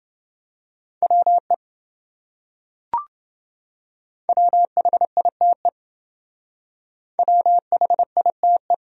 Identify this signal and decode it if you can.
Morse code, heard as a keyed tone, about 28 wpm, 700 Hz. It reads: WE WHITE WHITE